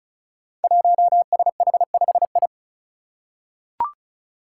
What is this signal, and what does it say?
Morse code, heard as a keyed tone, about 36 wpm, 700 Hz. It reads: 1SH5I
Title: Five Challenging Letter-Number Combinations 35wpm